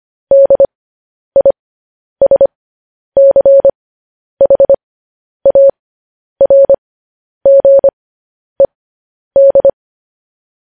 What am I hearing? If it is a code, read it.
Morse code, heard as a keyed tone, about 25 wpm, spaced as for 10 wpm, 550 Hz. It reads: DISCHARGED
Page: news-headlines-in-morse-code-at-10-wpm